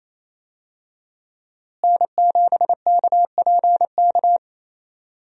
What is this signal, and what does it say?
Morse code, heard as a keyed tone, about 28 wpm, 700 Hz. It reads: N7KPK
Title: USA Call Signs - 5 Characters - Repeated 2x - 28wpm